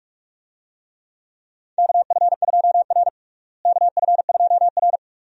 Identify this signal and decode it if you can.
Morse code, heard as a keyed tone, about 46 wpm, 700 Hz. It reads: KF2R KF2R
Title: USA Call Signs - 4 Characters - Repeated 2x - 45wpm